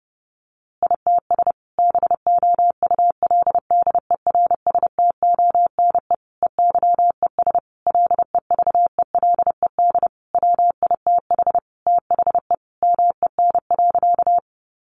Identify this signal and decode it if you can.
Morse code, heard as a keyed tone, about 30 wpm, 700 Hz. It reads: ITS BOULDERSTONE EYES LEVELED WITH THE MEN.
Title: A Sound Of Thunder 30wpm